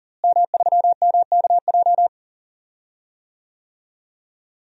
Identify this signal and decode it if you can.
Morse code, heard as a keyed tone, about 37 wpm, 700 Hz. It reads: M3MKJ
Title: DXCC-100 Call Signs - 5 Characters - Repeated 3x - 40wpm